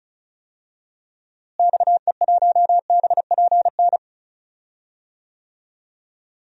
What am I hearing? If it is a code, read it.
Morse code, heard as a keyed tone, about 34 wpm, 700 Hz. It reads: XE1BPN
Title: DXCC-100 Call Signs - 6 Characters - Repeated 2x - 35wpm